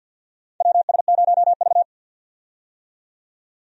Morse code, heard as a keyed tone, about 53 wpm, 700 Hz. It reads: WS0V
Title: USA Call Signs - 4 Characters - 50wpm